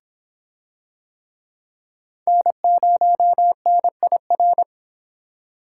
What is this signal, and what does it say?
Morse code, heard as a keyed tone, about 26 wpm, 700 Hz. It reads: N0NIR